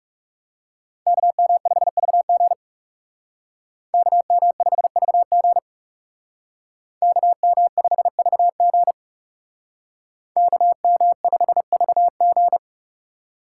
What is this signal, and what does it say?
Morse code, heard as a keyed tone, about 36 wpm, 700 Hz. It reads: KM5VG KM5VG KM5VG KM5VG
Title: Speed Racing - USA 2x2 Call Signs 30wpm